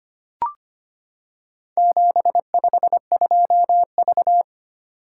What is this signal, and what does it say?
Morse code, heard as a keyed tone, about 25 wpm, 700 Hz. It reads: 752V
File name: practice / 25wpm / Four Letter-Number Combinations - 25wpm